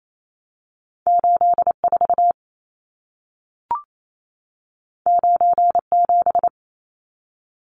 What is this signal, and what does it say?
Morse code, heard as a keyed tone, about 28 wpm, 700 Hz. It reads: 84 97
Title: Numbers - 2-Digit 28wpm